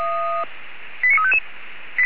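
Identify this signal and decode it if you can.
Five-tone selective call (ZVEI), heard as quick runs of five tones, tones of about 70 ms. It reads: 89355
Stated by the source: Five-tone ZVEI selective call received on an SDR by feuerwehr-orlamuende, fire brigade alerting.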